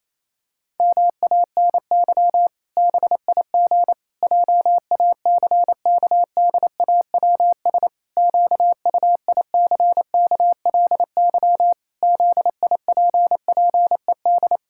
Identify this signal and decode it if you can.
Morse code, heard as a keyed tone, about 28 wpm, 700 Hz. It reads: MANY BIG JACKDAWS QUICKLY ZIPPED
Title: Pangram Challenge v2 - 28wpm